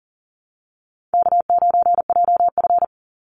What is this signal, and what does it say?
Morse code, heard as a keyed tone, about 40 wpm, 700 Hz. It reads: K9JF
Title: Easy Call Signs Repeated 3x - 40wpm